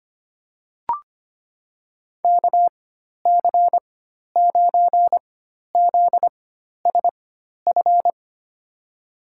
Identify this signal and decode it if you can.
Morse code, heard as a keyed tone, about 25 wpm, spaced as for 12 wpm, 700 Hz. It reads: KC9ZSF
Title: USA Call Signs - 6 Characters - 4x Character Spacing - 25wpm